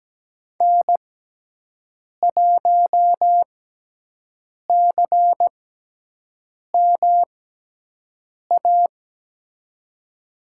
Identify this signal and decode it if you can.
Morse code, heard as a keyed tone, about 17 wpm, spaced as for 6 wpm, 700 Hz. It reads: N1CMA
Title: USA Call Signs - 5 Characters - 6x Character Spacing - 17wpm